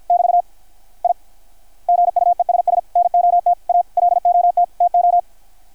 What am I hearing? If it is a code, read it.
Morse code, heard as a keyed tone, about 53 wpm, 660 Hz. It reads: <BT> I GUESS NOT A LOT TO
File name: archived / CRSnet